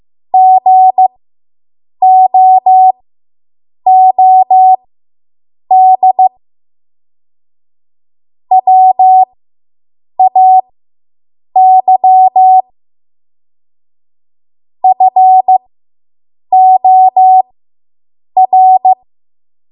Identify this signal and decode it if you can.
Morse code, heard as a keyed tone, about 15 wpm, spaced as for 7 wpm, 750 Hz. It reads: GOOD WAY FOR